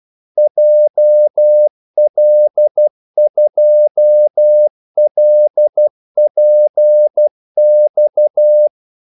Morse code, heard as a keyed tone, about 12 wpm, 600 Hz. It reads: JL2LPX
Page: 2 ekstra ordmellemrum mellem kaldesignalerne